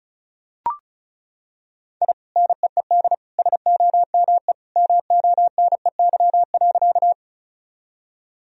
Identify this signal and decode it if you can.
Morse code, heard as a keyed tone, about 34 wpm, 700 Hz. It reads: I NEED SOME MONEY.
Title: Sentences from Top 1000 Words - Review Part 1 - 35wpm